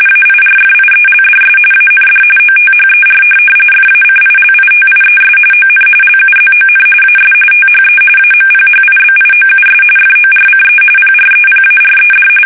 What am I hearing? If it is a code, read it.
RTTY, about 45.45 baud. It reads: AIRFIELDS AND OTHER FACILITIES ON LUZON AS THE AMPHIBIOUS FORCE PREPARED
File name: fsk850 / USS-Massachusetts